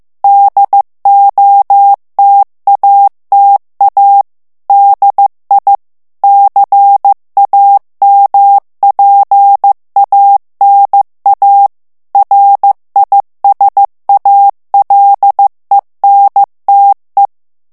Morse code, heard as a keyed tone, about 15 wpm, 800 Hz. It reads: DOTATA DI CAMPANA RISALENTE
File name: morse mp3 / ITALIEN texte clair 1 vitesse 14 mots minute